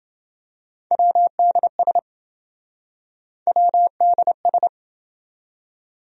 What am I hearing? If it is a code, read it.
Morse code, heard as a keyed tone, about 28 wpm, 700 Hz. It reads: WDS WDS